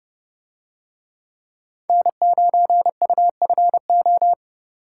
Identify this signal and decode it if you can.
Morse code, heard as a keyed tone, about 28 wpm, 700 Hz. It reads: N9UFO